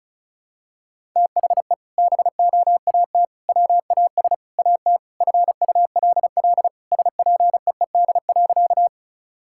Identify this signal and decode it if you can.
Morse code, heard as a keyed tone, about 34 wpm, 700 Hz. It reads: THE BOAT WAS AT FULL SPEED.
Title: Sentences from Top 1000 Words - Review Part 5 - 35wpm